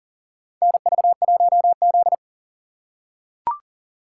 Morse code, heard as a keyed tone, about 43 wpm, 690 Hz. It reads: NV1Z
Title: USA Call Signs - 4 Characters - 2x Character Spacing - 40wpm